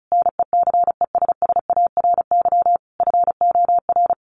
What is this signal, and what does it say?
Morse code, heard as a keyed tone, about 34 wpm, 700 Hz. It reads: NECESSARY FOR
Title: Sentences from Top 1300 Words 35wpm